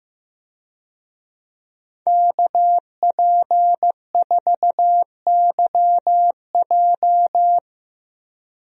Morse code, heard as a keyed tone, about 16 wpm, 700 Hz. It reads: KP4YJ
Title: DXCC-100 Call Signs - 5 Characters - Repeated 2x - 15wpm